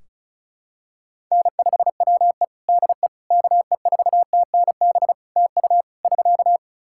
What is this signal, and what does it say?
Morse code, heard as a keyed tone, about 36 wpm, 700 Hz. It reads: N5WE DE KE4TNB TU <SK>
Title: K1USN SST QSO Practice 35wpm